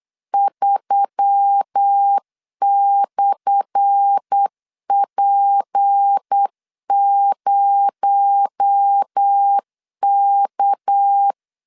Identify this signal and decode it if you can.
Morse code, heard as a keyed tone, about 8 wpm, 800 Hz. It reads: 3/P0K
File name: LEZIONI / 10 SIGNS / 10B INT 8wpm 3534